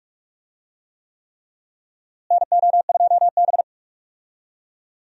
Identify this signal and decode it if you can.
Morse code, heard as a keyed tone, about 46 wpm, 700 Hz. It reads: NO2B
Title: USA Call Signs - 4 Characters - Repeated 3x - 45wpm